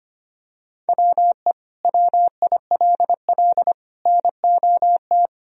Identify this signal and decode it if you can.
Morse code, heard as a keyed tone, about 25 wpm, 700 Hz. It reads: WE WILL NOT
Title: Sentences from Top 1000 Words - Review Part 1 - 25wpm